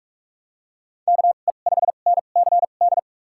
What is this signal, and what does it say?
Morse code, heard as a keyed tone, about 46 wpm, spaced as for 33 wpm, 700 Hz. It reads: KE5NCD
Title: USA Call Signs - 6 Characters - 2x Character Spacing - 45wpm